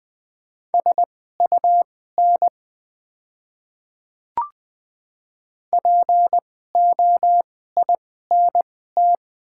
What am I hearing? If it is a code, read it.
Morse code, heard as a keyed tone, about 20 wpm, spaced as for 14 wpm, 700 Hz. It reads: SUN POINT
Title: 2x Character Spacing - Top 300 Words - 20wpm